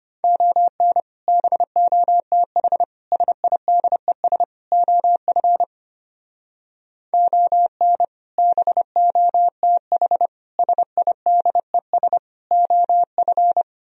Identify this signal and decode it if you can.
Morse code, heard as a keyed tone, about 32 wpm, 700 Hz. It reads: ON BOTH SIDES OF ON BOTH SIDES OF